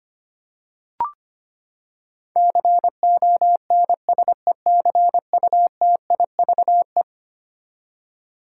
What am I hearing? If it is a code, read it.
Morse code, heard as a keyed tone, about 25 wpm, 700 Hz. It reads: CONSECUTIVE
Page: morse-code-ninja-practice-25wpm